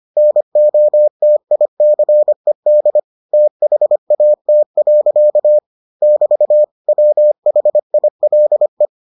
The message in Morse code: NOTICED THAT. <BT> WHILE

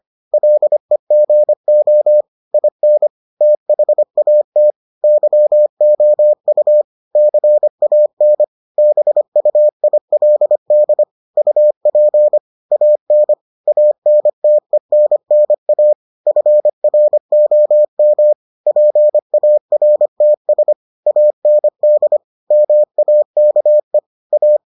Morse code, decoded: LEGO IN THAT YOU CAN BUILD UP AN ANTENNA FROM PARTS AND MAKE A

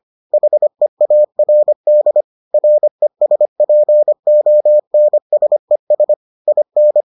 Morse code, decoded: HEARD RESPONSES IN

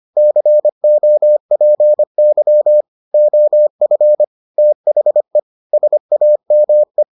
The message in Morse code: COPY OF THE SAME

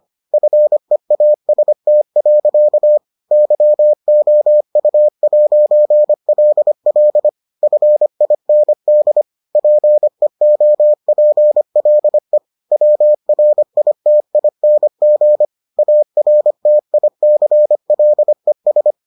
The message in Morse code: FEAST. YOU'LL FIND PEOPLE WRITING ARTICLES

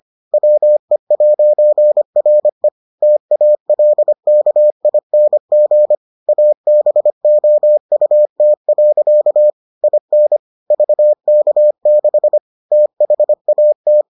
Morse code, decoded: WE'RE TALKING ABOUT. IN VK6 THAT

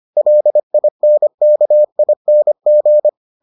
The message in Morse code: LINKING

